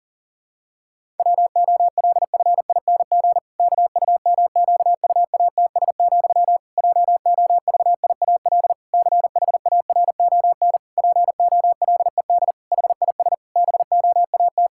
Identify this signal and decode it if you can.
Morse code, heard as a keyed tone, about 40 wpm, 700 Hz. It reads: WOLFING KUMQUATS, JOVIAL CHARON POLED HIS BOAT
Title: Pangram Challenge v2 - 40wpm